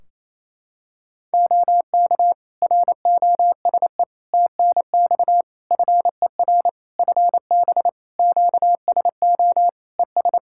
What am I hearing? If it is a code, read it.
Morse code, heard as a keyed tone, about 28 wpm, 700 Hz. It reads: OK ROSE TNX FER FB QSO ES
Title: Practice QSO - Exchange 4 - Ending - 28wpm